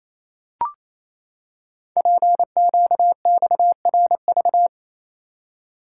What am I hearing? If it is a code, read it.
Morse code, heard as a keyed tone, about 28 wpm, 700 Hz. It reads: PQXRV
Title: Code Groups - 5 Letters - 28wpm